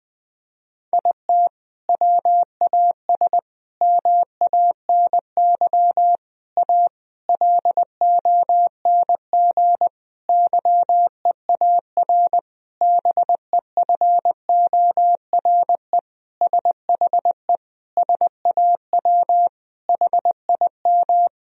Morse code, heard as a keyed tone, about 21 wpm, 700 Hz. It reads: IT WAS MANY A LONG YEAR BEFORE SHE SAW HIM
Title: Sentences from Top 1000 Words - Review Part 1 - 20wpm